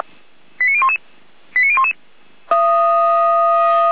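Five-tone selective call (ZVEI), heard as quick runs of five tones, tones of about 70 ms. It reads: 89011 89011
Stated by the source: ZVEI five-tone selective call received by SDR, recorded by Feuerwehr Orlamünde.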